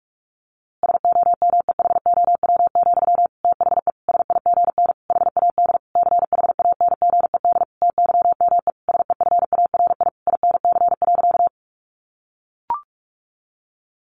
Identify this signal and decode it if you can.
Morse code, heard as a keyed tone, about 44 wpm, 700 Hz. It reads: SOMEHOW, THE SIGN HAD CHANGED TYME SEFARI INC.
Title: A Sound Of Thunder 45wpm